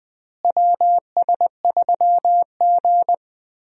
Morse code, heard as a keyed tone, about 20 wpm, 700 Hz. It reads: WS3G
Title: USA Call Signs - 4 Characters - Repeated 3x - 20wpm